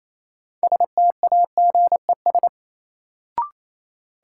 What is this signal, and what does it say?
Morse code, heard as a keyed tone, about 28 wpm, 700 Hz. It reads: STAGES
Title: Sets of 5 Words Encore - New Word Drill - 28wpm